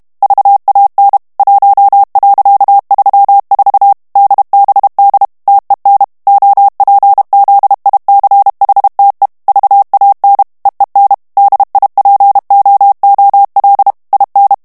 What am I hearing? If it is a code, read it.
Morse code, heard as a keyed tone, about 32 wpm, 800 Hz. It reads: VAN 1.34 DBD TEN OPZICHTE VAN EEN DIPOOL IN
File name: HOLLANDAIS texte clair 1 vitesse 30 mots minute